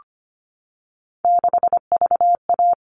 Morse code, heard as a keyed tone, about 25 wpm, 700 Hz. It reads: 6VA
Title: All Cut Numbers - 3-Digit 25wpm